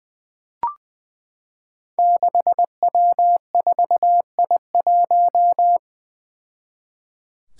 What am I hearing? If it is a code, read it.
Morse code, heard as a keyed tone, about 20 wpm, 700 Hz. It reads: 6W4I1
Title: Five Challenging Letter-Number Combinations 20wpm